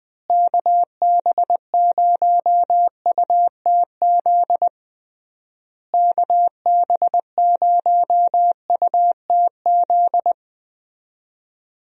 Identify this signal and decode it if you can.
Morse code, heard as a keyed tone, about 20 wpm, 700 Hz. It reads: KB0UTZ KB0UTZ
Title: USA Call Signs - 6 Characters - Repeated 3x - 20wpm